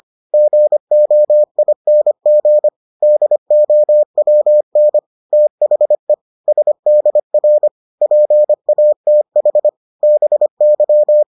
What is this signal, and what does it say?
Morse code, heard as a keyed tone, about 25 wpm, 600 Hz. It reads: GOING DOWN THE SDR PATH BY